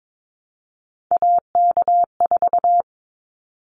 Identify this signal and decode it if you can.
Morse code, heard as a keyed tone, about 22 wpm, 700 Hz. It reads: AK4
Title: Callsign Prefixes for Top 25 DXCC - 22wpm